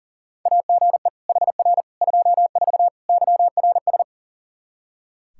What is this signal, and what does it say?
Morse code, heard as a keyed tone, about 40 wpm, 700 Hz. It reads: AGE HR 24 YRS
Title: Practice QSO - Exchange 3 - More Info - 40wpm